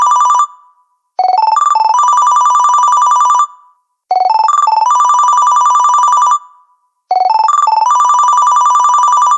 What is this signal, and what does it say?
There is no code.